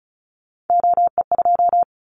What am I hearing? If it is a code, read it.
Morse code, heard as a keyed tone, about 34 wpm, 700 Hz. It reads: OE2
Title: Callsign Prefixes for Top 25 DXCC - Repeated 3x - 35wpm